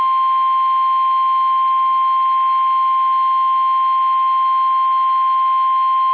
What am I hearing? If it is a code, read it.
Emergency Alert System (SAME) tones (the 1050 Hz attention tone). It is heard from 0.6 to 5.4 s.
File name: eas / KUSO / KUSO